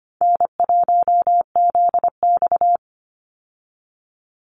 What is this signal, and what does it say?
Morse code, heard as a keyed tone, about 25 wpm, 700 Hz. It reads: N1ZX
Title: Easy Call Signs - 25wpm